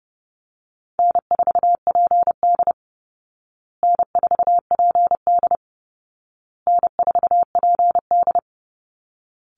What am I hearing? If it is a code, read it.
Morse code, heard as a keyed tone, about 30 wpm, 700 Hz. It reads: N4PD N4PD N4PD
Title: Easy Call Signs Repeated 3x - 30wpm